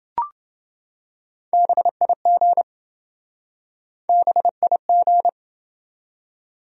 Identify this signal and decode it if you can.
Morse code, heard as a keyed tone, about 27 wpm, 700 Hz. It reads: BIG BIG